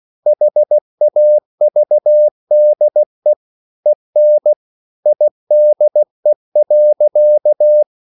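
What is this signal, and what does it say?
Morse code, heard as a keyed tone, about 16 wpm, 600 Hz. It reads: HAVDE EN IDE.
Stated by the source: Slow (QRS) Morse code at 16 wpm, from CW med Gnister.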